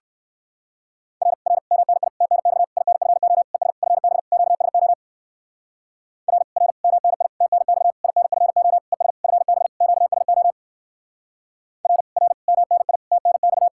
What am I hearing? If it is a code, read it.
Morse code, heard as a keyed tone, about 56 wpm, 700 Hz. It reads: R R MNI TNX INFO ES FB QSO R R MNI TNX INFO ES FB QSO R R MNI TNX